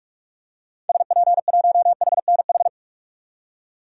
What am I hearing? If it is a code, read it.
Morse code, heard as a keyed tone, about 43 wpm, 700 Hz. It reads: SP1HNH
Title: DXCC-100 Call Signs - 6 Characters - 45wpm